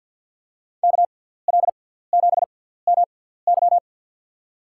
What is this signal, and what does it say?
Morse code, heard as a keyed tone, about 53 wpm, spaced as for 17 wpm, 700 Hz. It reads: KL7MY
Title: USA Call Signs - 5 Characters - 6x Character Spacing - 50wpm